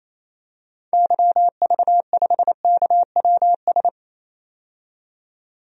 Morse code, heard as a keyed tone, about 28 wpm, 700 Hz. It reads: YV5KWS
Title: DXCC-100 Call Signs - 6 Characters - 28wpm